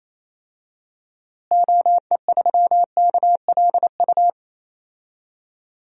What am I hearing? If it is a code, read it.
Morse code, heard as a keyed tone, about 28 wpm, 700 Hz. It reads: OE3KLU
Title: DXCC-100 Call Signs - 6 Characters - Repeated 3x - 28wpm